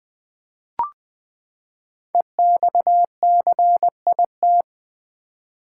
Morse code, heard as a keyed tone, about 20 wpm, 700 Hz. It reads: EXCIT
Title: Build A Word - 5 Letters At A Time - 10-Letter Words - 20wpm